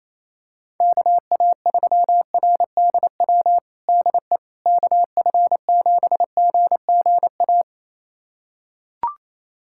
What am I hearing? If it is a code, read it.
Morse code, heard as a keyed tone, about 28 wpm, 700 Hz. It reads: KA3RDW DE KF7GGA
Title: USA Callsign DE Callsign 28wpm